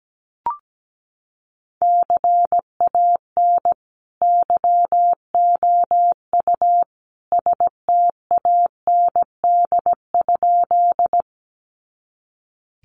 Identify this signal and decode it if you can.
Morse code, heard as a keyed tone, about 17 wpm, 700 Hz. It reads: CAN YOU STAND?
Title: Sentences from Top 300 Words 17wpm